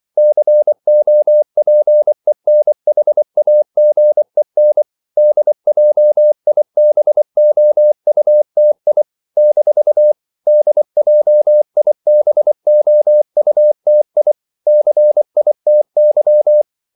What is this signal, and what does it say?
Morse code, heard as a keyed tone, about 24 wpm, 600 Hz. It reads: COPENHAGEN DJIBOUTI - DJIBOUTI CITY
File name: CW / Practice / NationalCapitals24